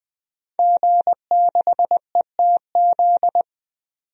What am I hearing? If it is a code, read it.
Morse code, heard as a keyed tone, about 20 wpm, 700 Hz. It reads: G6ETZ
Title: DXCC-100 Call Signs - 5 Characters - 20wpm